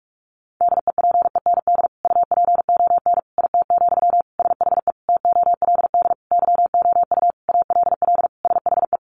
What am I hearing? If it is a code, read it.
Morse code, heard as a keyed tone, about 44 wpm, 700 Hz. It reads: DEPEND UPON IT, SHE TOLD YOU ALL SHE
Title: Sentences from Top 1300 Words 45wpm